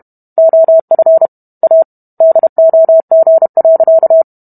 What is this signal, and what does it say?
Morse code, heard as a keyed tone, about 32 wpm, 650 Hz. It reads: OF A DOG.